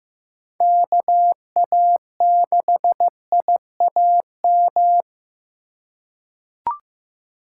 Morse code, heard as a keyed tone, about 15 wpm, 700 Hz. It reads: KA6IAM
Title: USA Call Signs - 6 Characters - 2x Character Spacing - 15wpm